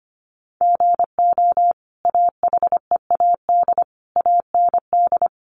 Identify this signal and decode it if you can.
Morse code, heard as a keyed tone, about 25 wpm, 700 Hz. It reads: GO AHEAD AND
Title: Sentences from Top 1100 Words 25wpm